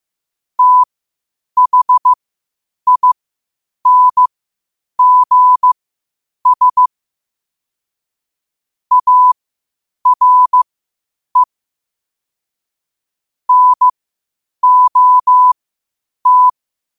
Morse code, heard as a keyed tone, about 15 wpm, spaced as for 8 wpm, 1000 Hz. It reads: THINGS ARE NOT
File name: qotdfiles / 8wpm